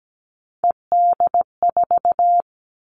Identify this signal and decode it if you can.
Morse code, heard as a keyed tone, about 17 wpm, 700 Hz. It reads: ED4
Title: All Cut Numbers - 3-Digit 17wpm